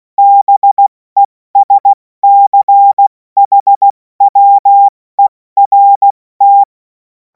Morse code, heard as a keyed tone, about 16 wpm, 800 Hz. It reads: BESCHWERT
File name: Morsekurs / Witze / Witz 28. April 2026 70 F80